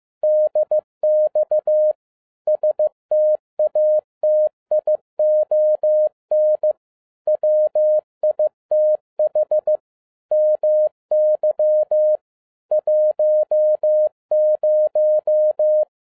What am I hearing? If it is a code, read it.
Morse code, heard as a keyed tone, about 15 wpm, 610 Hz. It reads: DX STATION WITH MY 10